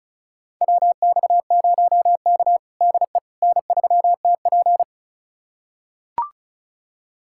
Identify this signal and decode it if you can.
Morse code, heard as a keyed tone, about 34 wpm, 700 Hz. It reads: WX0K DE N3TP
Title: USA Callsign DE Callsign - 4 Characters - 35wpm